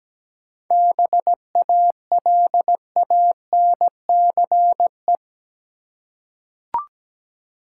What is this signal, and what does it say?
Morse code, heard as a keyed tone, about 17 wpm, 700 Hz. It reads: BALANCE
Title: Seven Letter Words - 17wpm